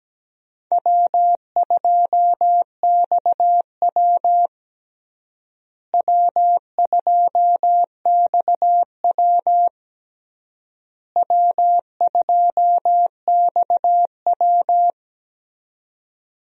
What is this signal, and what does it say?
Morse code, heard as a keyed tone, about 17 wpm, 700 Hz. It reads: W2XW W2XW W2XW
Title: USA Call Signs - 4 Characters - Repeated 3x - 17wpm